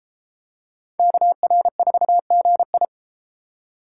Morse code, heard as a keyed tone, about 29 wpm, 700 Hz. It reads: KR4GI